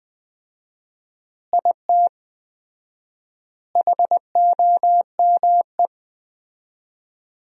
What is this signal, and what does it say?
Morse code, heard as a keyed tone, about 20 wpm, 700 Hz. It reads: IT HOME
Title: 4x Word-Spacing - Sentences from Top 500 Words - 20wpm